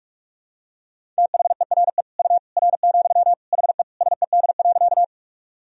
Morse code, heard as a keyed tone, about 42 wpm, 700 Hz. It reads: THERE U R, HE SED.
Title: Sentences from Top 100 Words and Abbreviations 45wpm